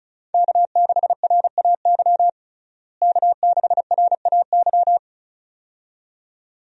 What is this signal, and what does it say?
Morse code, heard as a keyed tone, about 36 wpm, 700 Hz. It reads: K6RAY K6RAY
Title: USA Call Signs - 5 Characters - Repeated 2x - 35wpm